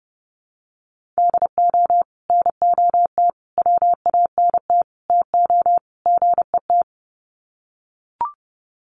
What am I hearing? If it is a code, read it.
Morse code, heard as a keyed tone, about 30 wpm, 700 Hz. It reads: DO NOT WANT TO GET
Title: Sets of 5 Words 30wpm